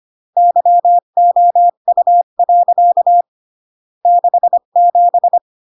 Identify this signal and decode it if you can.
Morse code, heard as a keyed tone, about 25 wpm, 700 Hz. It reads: YOU. 67